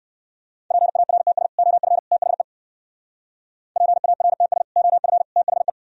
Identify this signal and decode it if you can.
Morse code, heard as a keyed tone, about 64 wpm, 700 Hz. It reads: PARTS OF THE PARTS OF THE